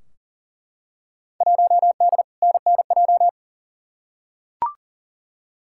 Morse code, heard as a keyed tone, about 40 wpm, 700 Hz. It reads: 1D NNJ
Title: Field Day Exchanges Encore - 40wpm